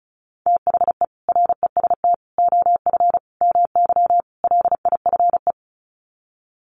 Morse code, heard as a keyed tone, about 34 wpm, 700 Hz. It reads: THE REST OF MY LIFE
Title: Sets of 5 Words 35wpm